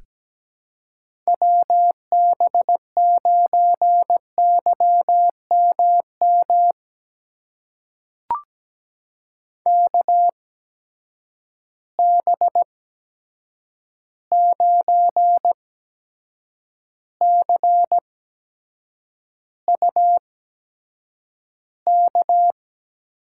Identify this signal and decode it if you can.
Morse code, heard as a keyed tone, about 17 wpm, spaced as for 7 wpm, 700 Hz. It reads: WB9YMM KB9CUK
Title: USA Call Signs - 6 Characters - 8x Character Spacing - 17wpm